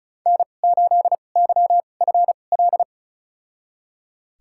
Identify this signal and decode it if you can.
Morse code, heard as a keyed tone, about 36 wpm, spaced as for 25 wpm, 700 Hz. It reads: N8YFL
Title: USA Call Signs - 2x Character Spacing - 35wpm